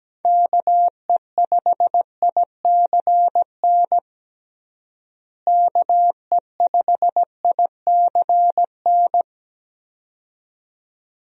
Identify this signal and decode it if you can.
Morse code, heard as a keyed tone, about 17 wpm, 700 Hz. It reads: KE5ICN KE5ICN
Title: USA Call Signs - 6 Characters - Repeated 2x - 17wpm